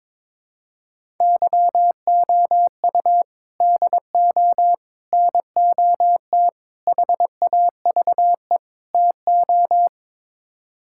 Morse code, heard as a keyed tone, about 22 wpm, 700 Hz. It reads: YOU DO NOT HAVE TO
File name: practice / 22wpm / Build A Phrase - 5-Word Phrases - 22wpm